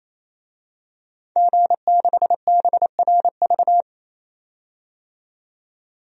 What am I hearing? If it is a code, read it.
Morse code, heard as a keyed tone, about 28 wpm, 700 Hz. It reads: G6BRV
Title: DXCC-100 Call Signs - 5 Characters - 28wpm